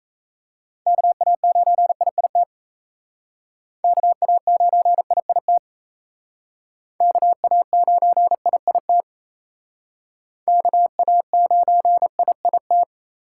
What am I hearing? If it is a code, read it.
Morse code, heard as a keyed tone, about 35 wpm, 700 Hz. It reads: KA9IIT KA9IIT KA9IIT KA9IIT
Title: Speed Racing - USA 2x3 Call Signs 28wpm